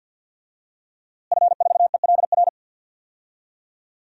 Morse code, heard as a keyed tone, about 52 wpm, 700 Hz. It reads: F4ELR
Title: DXCC-100 Call Signs - 5 Characters - Repeated 3x - 50wpm